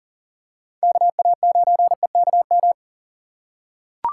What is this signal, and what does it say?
Morse code, heard as a keyed tone, about 38 wpm, 700 Hz. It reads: KA9EKM
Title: USA Call Signs - 3x Character Spacing - 40wpm